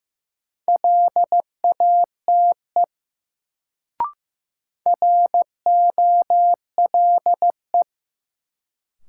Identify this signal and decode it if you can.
Morse code, heard as a keyed tone, about 15 wpm, 700 Hz. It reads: LATE ROLE
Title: Sets of 3 Words - New Word Drill - 15wpm